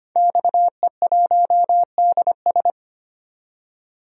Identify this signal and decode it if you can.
Morse code, heard as a keyed tone, about 25 wpm, 700 Hz. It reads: XE1DS
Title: DXCC-100 Call Signs - 5 Characters - 25wpm